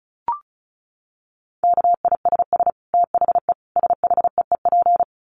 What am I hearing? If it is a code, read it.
Morse code, heard as a keyed tone, about 34 wpm, 700 Hz. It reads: KISS THE SHEEP
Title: Dit Challenge 35wpm